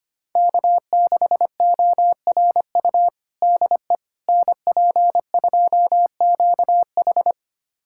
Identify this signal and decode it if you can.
Morse code, heard as a keyed tone, about 25 wpm, 700 Hz. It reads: K6ORU DE NP2QH
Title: USA Callsign DE Callsign - 5 Characters - 25wpm